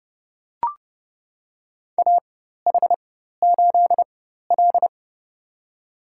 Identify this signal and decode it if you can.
Morse code, heard as a keyed tone, about 30 wpm, spaced as for 14 wpm, 700 Hz. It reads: AH8L
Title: USA Call Signs - 4 Characters - 4x Character Spacing - 30wpm